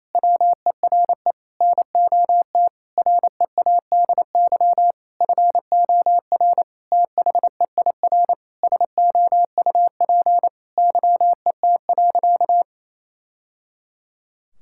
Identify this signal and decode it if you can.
Morse code, heard as a keyed tone, about 27 wpm, 700 Hz. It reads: WERE NOT READY FOR THEIR SOUP YET.
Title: Sentences from Top 1000 Words - Review Part 6 - 28wpm